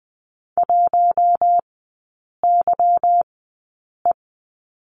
Morse code, heard as a keyed tone, about 20 wpm, spaced as for 8 wpm, 700 Hz. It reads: 1YE
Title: 145 Lesson 24e - Letter y - Three Characters 20wpm